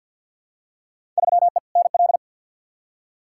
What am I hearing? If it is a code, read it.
Morse code, heard as a keyed tone, about 52 wpm, 700 Hz. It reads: 3E NL
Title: Field Day Exchanges Encore - 50wpm